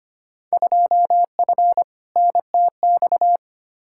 Morse code, heard as a keyed tone, about 25 wpm, 700 Hz. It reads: 2F NTX
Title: Field Day Exchanges - 25wpm